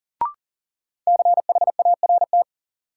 Morse code, heard as a keyed tone, about 40 wpm, 700 Hz. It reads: CHART
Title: Sentences from Top 1300 Words - New Word Drill Alt - 40wpm